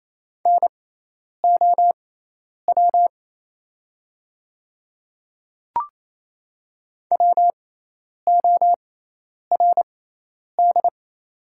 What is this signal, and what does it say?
Morse code, heard as a keyed tone, about 28 wpm, spaced as for 10 wpm, 700 Hz. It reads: NOW WORD
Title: 6x Character Spacing - Top 100 Words - 28wpm